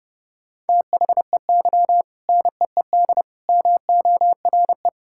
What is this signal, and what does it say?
Morse code, heard as a keyed tone, about 32 wpm, 700 Hz. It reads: THEY NEED MORE
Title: Sentences from Top 1000 Words - Review Part 6 - 30wpm